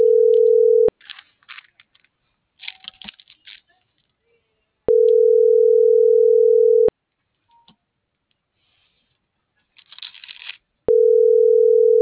Ringback tone (North America), 440 + 480 Hz.